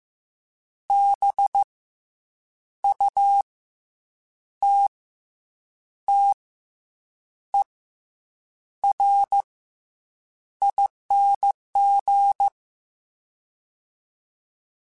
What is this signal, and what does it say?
Morse code, heard as a keyed tone, about 15 wpm, 800 Hz. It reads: B U T T E R ING
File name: Special INGs / Special INGs-mp3s / SlowToFaster ing @13